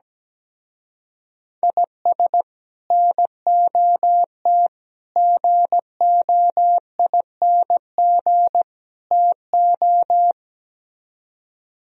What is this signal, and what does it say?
Morse code, heard as a keyed tone, about 17 wpm, 700 Hz. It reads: IS NOT GOING TO
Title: Speed-Racing - Sets of 4 Words 17wpm